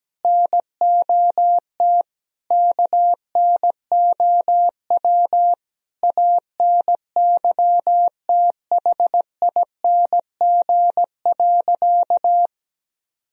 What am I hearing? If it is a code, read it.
Morse code, heard as a keyed tone, about 17 wpm, 700 Hz. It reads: NOT KNOW ANYTHING.
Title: Sentences from Top 1000 Words - Review Part 2 - 17wpm